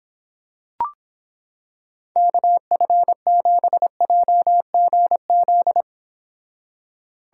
Morse code, heard as a keyed tone, about 25 wpm, 700 Hz. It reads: KF7JGZ